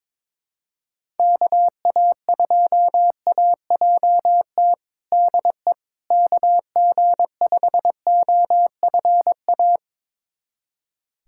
Morse code, heard as a keyed tone, about 22 wpm, 700 Hz. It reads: KA2AJT DE KG5OFA
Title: USA Callsign DE Callsign - 6 Characters - 22wpm